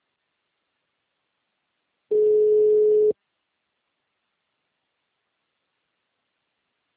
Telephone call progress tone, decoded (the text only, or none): ringback tone